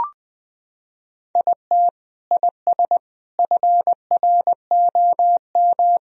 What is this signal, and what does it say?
Morse code, heard as a keyed tone, about 21 wpm, 700 Hz. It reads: IT IS FROM
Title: Sentences from Top 1000 Words - Review Part 1 - 20wpm